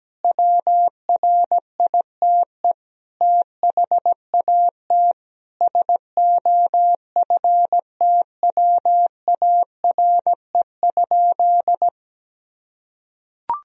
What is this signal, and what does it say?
Morse code, heard as a keyed tone, about 17 wpm, 700 Hz. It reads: WRITE THAT SOFTWARE?
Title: Sentences from Top 1000 Words - Review Part 3 - 17wpm